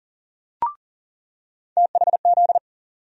Morse code, heard as a keyed tone, about 40 wpm, 700 Hz. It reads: THZ
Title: USA Call Sign Postfixes - 3 Letters 40wpm